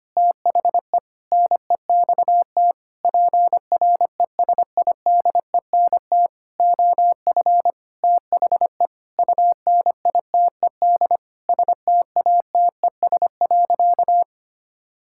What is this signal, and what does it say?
Morse code, heard as a keyed tone, about 25 wpm, 700 Hz. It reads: THE NEXT PRESIDENT OF THE UNITED STATES.
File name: practice / 25wpm / Sentences with 9-Word Phrases - 25wpm